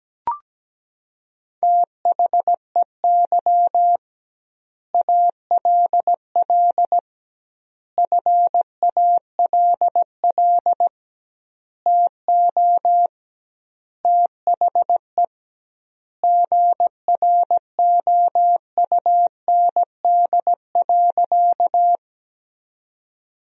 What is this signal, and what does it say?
Morse code, heard as a keyed tone, about 17 wpm, 700 Hz. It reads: THEY ALL FALL TO THE GROUND.
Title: 2x Word-Spacing - Sentences from Top 500 Words - 17wpm